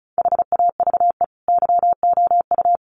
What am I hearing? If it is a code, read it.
Morse code, heard as a keyed tone, about 34 wpm, 700 Hz. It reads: HAVE YOU
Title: Sentences from Top 1100 Words 35wpm